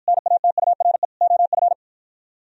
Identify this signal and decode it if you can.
Morse code, heard as a keyed tone, about 52 wpm, 700 Hz. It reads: NATURE OF